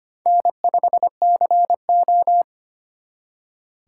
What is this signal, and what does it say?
Morse code, heard as a keyed tone, about 32 wpm, 700 Hz. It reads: N5CO